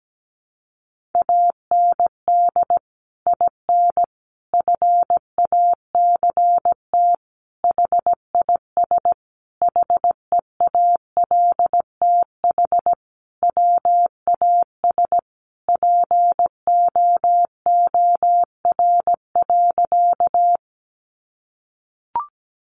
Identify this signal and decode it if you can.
Morse code, heard as a keyed tone, about 17 wpm, 700 Hz. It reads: AND IN FACT HIS HEALTH WAS POOR.
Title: Sentences from Top 1000 Words 17wpm